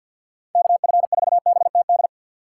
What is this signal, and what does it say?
Morse code, heard as a keyed tone, about 53 wpm, 700 Hz. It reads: KF4BTD
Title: USA Call Signs - 6 Characters - 4x Character Spacing - 50wpm